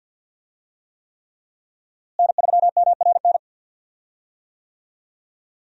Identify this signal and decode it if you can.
Morse code, heard as a keyed tone, about 51 wpm, 700 Hz. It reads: N3MRN
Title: USA Call Signs - 5 Characters - Repeated 3x - 50wpm